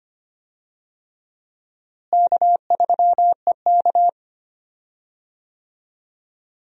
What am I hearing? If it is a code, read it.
Morse code, heard as a keyed tone, about 25 wpm, 700 Hz. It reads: K3EK
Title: USA Call Signs - 4 Characters - Repeated 3x - 25wpm